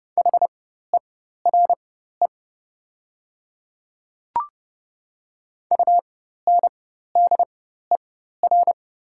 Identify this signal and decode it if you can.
Morse code, heard as a keyed tone, about 30 wpm, spaced as for 14 wpm, 690 Hz. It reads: HERE UNDER
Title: 4x Character Spacing - Top 200 Words - 30wpm